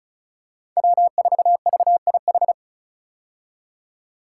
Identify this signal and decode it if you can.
Morse code, heard as a keyed tone, about 36 wpm, 690 Hz. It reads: W4VIH
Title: Code Groups - 5 Letter:Numbers - 35wpm